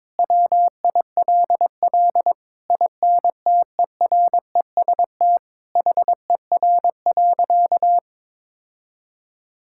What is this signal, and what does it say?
Morse code, heard as a keyed tone, about 22 wpm, 700 Hz. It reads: WILL INTEREST HER.
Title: Sentences from Top 1000 Words - Review Part 5 - 22wpm